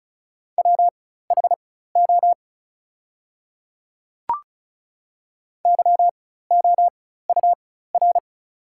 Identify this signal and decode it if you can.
Morse code, heard as a keyed tone, about 36 wpm, spaced as for 16 wpm, 700 Hz. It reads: WHO YOUR
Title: 4x Character Spacing - Top 100 Words - 35wpm